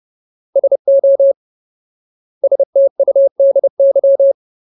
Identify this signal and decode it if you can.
Morse code, heard as a keyed tone, about 30 wpm, 550 Hz. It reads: SO STUDY
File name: Ultimate Word List / 30WPM Ultimate